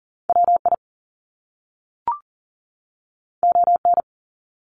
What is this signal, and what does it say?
Morse code, heard as a keyed tone, about 40 wpm, 700 Hz. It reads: WI ON
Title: Top Two Letter Combinations in Words 40wpm